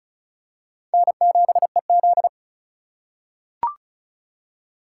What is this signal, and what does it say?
Morse code, heard as a keyed tone, about 34 wpm, 700 Hz. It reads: N7EZ
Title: USA Call Signs - 4 Characters - 2x Character Spacing - 35wpm